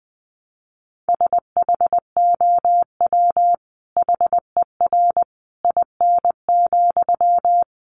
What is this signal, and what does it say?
Morse code, heard as a keyed tone, about 20 wpm, 700 Hz. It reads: SHOW HER IN,
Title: Sentences from Top 1100 Words 20wpm